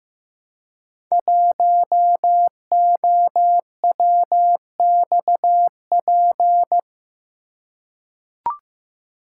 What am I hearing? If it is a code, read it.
Morse code, heard as a keyed tone, about 15 wpm, 700 Hz. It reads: 1OWXP
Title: Code Groups - 5 Letter:Numbers - 15wpm